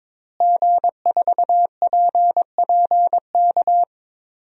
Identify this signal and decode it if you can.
Morse code, heard as a keyed tone, about 22 wpm, 700 Hz. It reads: G4PPK
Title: DXCC-100 Call Signs - 5 Characters - Repeated 2x - 22wpm